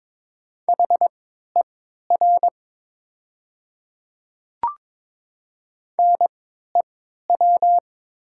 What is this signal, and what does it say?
Morse code, heard as a keyed tone, about 22 wpm, spaced as for 13 wpm, 690 Hz. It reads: HER NEW
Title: 3x Character Spacing - Top 200 Words - 22wpm